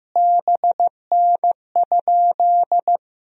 Morse code, heard as a keyed tone, about 15 wpm, 700 Hz. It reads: BN?
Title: Sentences from Top 400 Words and Abbreviations 15wpm